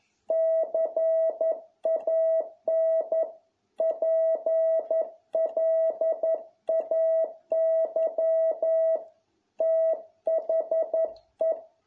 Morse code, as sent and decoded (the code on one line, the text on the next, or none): -.-. .- -. / .--. .-.. .- -.-- / - .... .
CAN PLAY THE